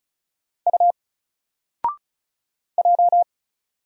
Morse code, heard as a keyed tone, about 36 wpm, 700 Hz. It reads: U J
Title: Single Letter-Number 35wpm